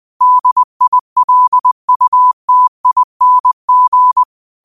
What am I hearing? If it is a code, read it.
Morse code, heard as a keyed tone, about 20 wpm, 1000 Hz. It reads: DILUTING
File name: qotdfiles / 20wpm